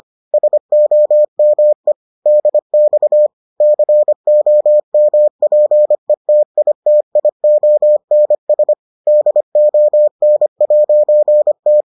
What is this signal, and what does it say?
Morse code, heard as a keyed tone, about 25 wpm, 600 Hz. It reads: SOME DX COMPETITIONS DON'T